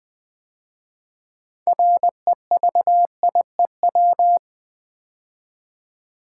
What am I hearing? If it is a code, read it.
Morse code, heard as a keyed tone, about 20 wpm, 700 Hz. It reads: REVIEW
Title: 6 Letter Words - First 3 Letters Given - 20wpm